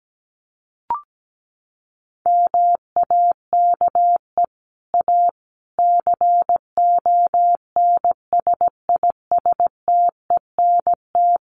Morse code, heard as a keyed tone, about 17 wpm, 700 Hz. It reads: MAKE A CONSISTENT
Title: Sentences from Top 1500 Words 17wpm